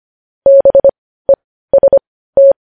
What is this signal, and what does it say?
Morse code, heard as a keyed tone, about 25 wpm, spaced as for 15 wpm, 550 Hz. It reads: BEST
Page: News headlines at 15